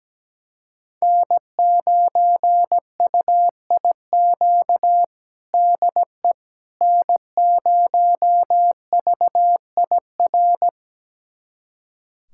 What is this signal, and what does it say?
Morse code, heard as a keyed tone, about 17 wpm, 700 Hz. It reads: N9UIQ DE N0VIR
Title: USA Callsign DE Callsign - 5 Characters - 17wpm